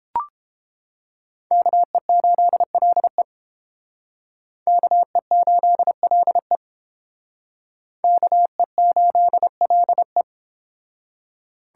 Morse code, heard as a keyed tone, about 29 wpm, 700 Hz. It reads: KE8LE KE8LE KE8LE